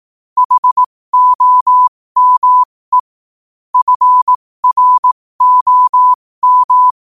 Morse code, heard as a keyed tone, about 18 wpm, spaced as for 15 wpm, 1000 Hz. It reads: HOME FROM
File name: qotdfiles / 15wpm